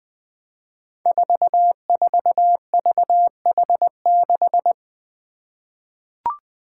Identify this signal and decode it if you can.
Morse code, heard as a keyed tone, about 20 wpm, 700 Hz. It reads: 44VH6
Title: Five Challenging Letter-Number Combinations 20wpm